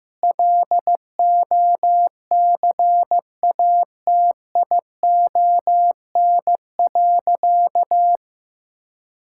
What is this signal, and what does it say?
Morse code, heard as a keyed tone, about 15 wpm, 700 Hz. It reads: LOCATION.
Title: Sentences from Top 1000 Words - Review Part 6 - 15wpm